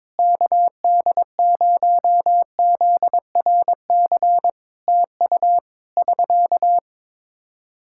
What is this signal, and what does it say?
Morse code, heard as a keyed tone, about 22 wpm, 700 Hz. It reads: KD0ZRC TU <SK>
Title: Practice QSO - Exchange 4 - Ending - 22wpm